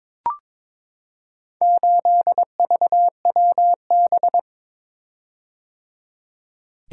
Morse code, heard as a keyed tone, about 22 wpm, 700 Hz. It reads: 8VWB
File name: practice / 22wpm / Four Letter-Number Combinations - 22wpm